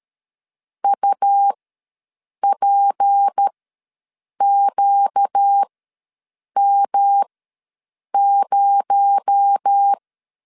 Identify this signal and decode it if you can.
Morse code, heard as a keyed tone, about 13 wpm, spaced as for 7 wpm, 800 Hz. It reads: UPQM0
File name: LEZIONI / 6 PKX0 / 6 PKX0 ws 12wpm 3448